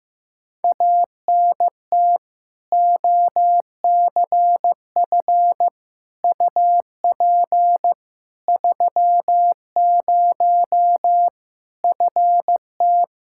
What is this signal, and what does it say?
Morse code, heard as a keyed tone, about 15 wpm, 700 Hz. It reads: ANT OCF UP 30 FT
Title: Practice QSO - Exchange 2 - Info - 15wpm